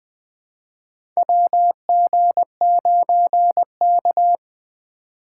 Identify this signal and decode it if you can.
Morse code, heard as a keyed tone, about 20 wpm, 700 Hz. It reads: WG9K
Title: USA Call Signs - 2x Character Spacing - 20wpm